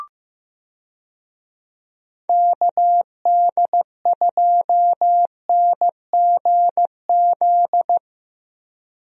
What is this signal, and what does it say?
Morse code, heard as a keyed tone, about 15 wpm, 700 Hz. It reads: KD2NGZ
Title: USA Call Signs - 6 Characters - Repeated 2x - 15wpm